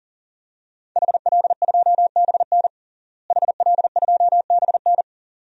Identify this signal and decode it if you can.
Morse code, heard as a keyed tone, about 37 wpm, 700 Hz. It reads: HL2BN HL2BN
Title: DXCC-100 Call Signs - 5 Characters - Repeated 2x - 40wpm